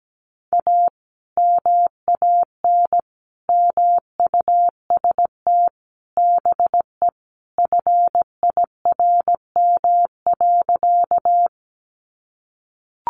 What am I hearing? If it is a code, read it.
Morse code, heard as a keyed tone, about 17 wpm, 700 Hz. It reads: A MAN MUST BE FIRM.
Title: Sentences from Top 1400 Words 17wpm